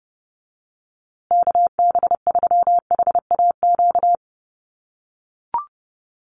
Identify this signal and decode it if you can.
Morse code, heard as a keyed tone, about 30 wpm, 700 Hz. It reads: KB3HAQ
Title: USA Call Signs Repeated 2x 30wpm